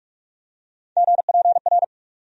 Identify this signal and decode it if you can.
Morse code, heard as a keyed tone, about 46 wpm, 700 Hz. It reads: GPR
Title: USA Call Sign Postfixes - 3 Letters 45wpm